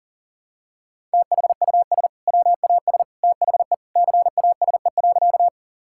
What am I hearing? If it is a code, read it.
Morse code, heard as a keyed tone, about 40 wpm, 700 Hz. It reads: THUS WAS THE CASE.
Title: Sentences from Top 1000 Words - Review Part 5 - 40wpm